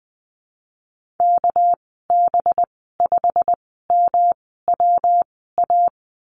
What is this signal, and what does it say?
Morse code, heard as a keyed tone, about 20 wpm, spaced as for 15 wpm, 700 Hz. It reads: KB5MWA
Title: 207 Lesson 33g - Letter j - Call Signs 20wpm